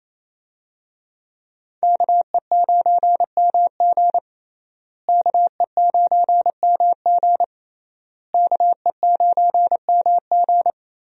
Morse code, heard as a keyed tone, about 29 wpm, 700 Hz. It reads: KE9MG KE9MG KE9MG
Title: USA Call Signs - 5 Characters - Repeated 3x - 28wpm